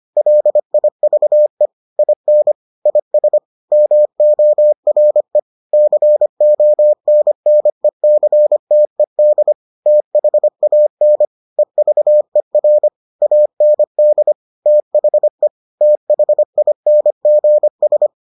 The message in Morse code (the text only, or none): LIVE IN IS MORE CONNECTED THAN EVER AND THE THINGS